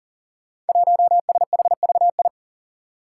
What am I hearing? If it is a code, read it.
Morse code, heard as a keyed tone, about 40 wpm, 700 Hz. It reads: 1SHVI
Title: Five Challenging Letter-Number Combinations 40wpm